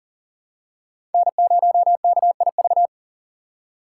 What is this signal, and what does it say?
Morse code, heard as a keyed tone, about 39 wpm, 700 Hz. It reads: N0KIV
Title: USA Call Signs - 5 Characters - 6x Character Spacing - 40wpm